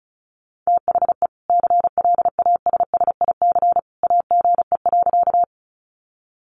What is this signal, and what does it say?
Morse code, heard as a keyed tone, about 34 wpm, 700 Hz. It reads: THE CLASSIC AGE.
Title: Sentences from Top 1000 Words 35wpm